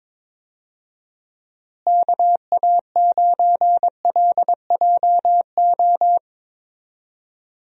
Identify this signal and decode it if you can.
Morse code, heard as a keyed tone, about 22 wpm, 700 Hz. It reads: KA9LJO
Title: USA Call Signs - 6 Characters - Repeated 2x - 22wpm